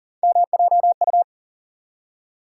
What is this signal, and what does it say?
Morse code, heard as a keyed tone, about 40 wpm, 700 Hz. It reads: MJU
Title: USA Call Sign Postfixes - 3 Letters 40wpm